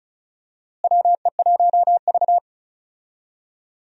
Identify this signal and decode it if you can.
Morse code, heard as a keyed tone, about 34 wpm, 700 Hz. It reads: WE1V
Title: USA Call Signs - 4 Characters - Repeated 3x - 35wpm